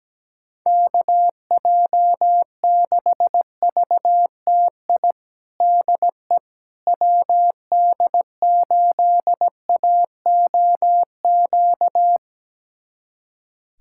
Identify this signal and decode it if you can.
Morse code, heard as a keyed tone, about 17 wpm, 700 Hz. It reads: KJ6VTI DE WD8AOQ
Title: USA Callsign DE Callsign - 6 Characters - 17wpm